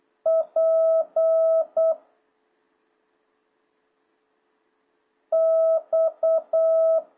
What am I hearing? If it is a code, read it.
Morse code, heard as a keyed tone, about 8 wpm, 660 Hz. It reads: P X